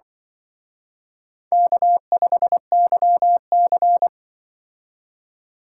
Morse code, heard as a keyed tone, about 25 wpm, 700 Hz. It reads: K5YC
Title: Speed Racing - USA 1x2 Call Signs 20wpm